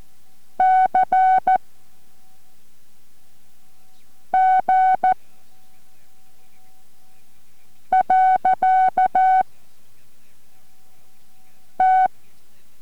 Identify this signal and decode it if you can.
Morse code, heard as a keyed tone, about 14 wpm, 750 Hz. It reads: C G . T